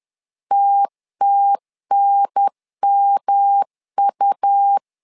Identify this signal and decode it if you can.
Morse code, heard as a keyed tone, about 11 wpm, 800 Hz. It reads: TTNMU
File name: LEZIONI / 3 MWRO / 3 M 10wpm 1828